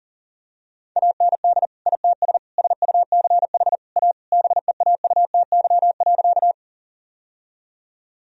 Morse code, heard as a keyed tone, about 40 wpm, 700 Hz. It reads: AND ITS SUCH A BEAUTY.
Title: Sentences from Top 1000 Words - Review Part 2 - 40wpm